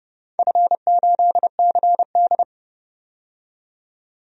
Morse code, heard as a keyed tone, about 30 wpm, 700 Hz. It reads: F8CD
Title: DXCC-100 Call Signs - 4 Characters - 30wpm